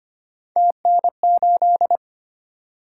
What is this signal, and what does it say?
Morse code, heard as a keyed tone, about 25 wpm, 700 Hz. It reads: TN8
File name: Three Letter-Number Combinations - 25wpm